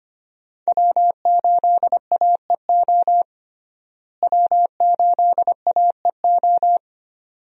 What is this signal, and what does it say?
Morse code, heard as a keyed tone, about 25 wpm, 700 Hz. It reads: W8AEO W8AEO
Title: USA Call Signs - 5 Characters - Repeated 2x - 25wpm